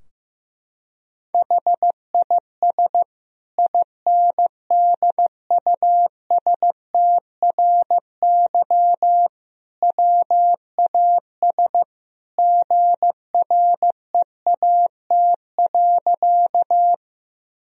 Morse code, heard as a keyed tone, about 16 wpm, 700 Hz. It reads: HIS INDUSTRY WAS GREAT.
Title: Sentences from Top 1000 Words - Review Part 4 - 15wpm